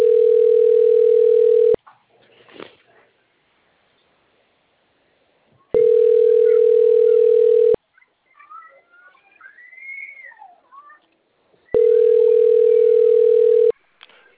Ringback tone (North America), 440 + 480 Hz.
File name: out- - - - - .15129